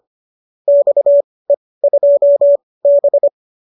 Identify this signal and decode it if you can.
Morse code, heard as a keyed tone, about 25 wpm, spaced as for 18 wpm, 570 Hz. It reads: XE2B